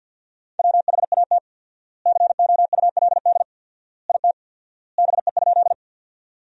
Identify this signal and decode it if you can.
Morse code, heard as a keyed tone, about 50 wpm, 700 Hz. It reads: WHAT COULD IT BE?
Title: 4x Word-Spacing - Sentences from Top 100 Words - 50wpm